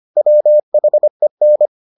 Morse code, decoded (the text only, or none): WHEN